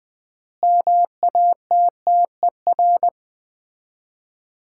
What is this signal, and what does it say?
Morse code, heard as a keyed tone, about 20 wpm, 700 Hz. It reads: MATTER
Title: 3x Word-Spacing - Sets of 4 Words - 20wpm